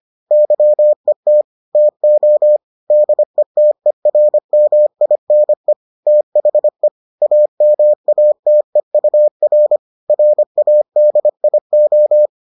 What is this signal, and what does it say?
Morse code, heard as a keyed tone, about 25 wpm, 600 Hz. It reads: YET TO DETERMINE THE AMATEUR RADIO